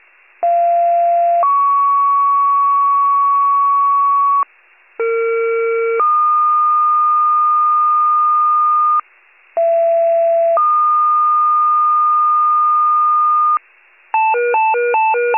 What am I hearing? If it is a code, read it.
Two-tone page, heard as a one-second tone, then a three-second tone, about 690 then 1120 Hz, about 460 then 1220 Hz, about 670 then 1220 Hz. It is heard from 0.4 to 13.6 s.